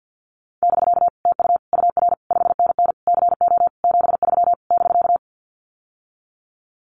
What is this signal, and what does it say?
Morse code, heard as a keyed tone, about 50 wpm, 700 Hz. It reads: <BK> TU UR 5NN CO 73 <BK>
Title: Common QSO Phrases Encore 50wpm